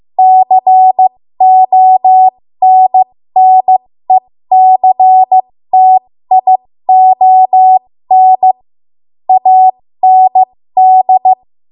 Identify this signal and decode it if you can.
Morse code, heard as a keyed tone, about 15 wpm, spaced as for 13 wpm, 750 Hz. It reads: CONNECTION AND
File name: Morse / 13WPM